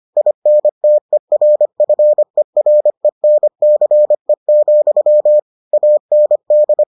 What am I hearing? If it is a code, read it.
Morse code, heard as a keyed tone, about 25 wpm, 600 Hz. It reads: INTERFERENCE, AND